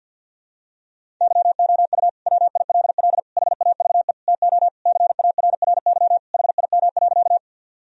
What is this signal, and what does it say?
Morse code, heard as a keyed tone, about 52 wpm, 700 Hz. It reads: YOU WILL HAVE TO CARRY HIM.
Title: Sentences from Top 1000 Words - Review Part 1 - 50wpm